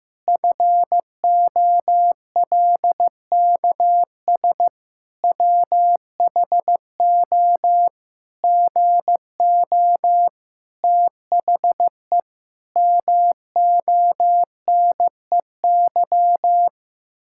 Morse code, heard as a keyed tone, about 15 wpm, 700 Hz. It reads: FOLKS WHO GO THE MONEY
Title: Pangram Challenge v2 - 15wpm